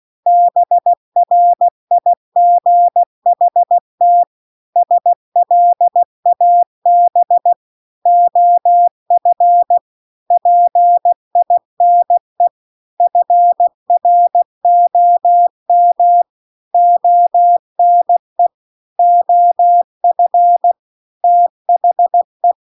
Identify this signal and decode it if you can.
Morse code, heard as a keyed tone, about 16 wpm, 700 Hz. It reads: BRIGHT SLAB OF PINE FROM ONE OF THE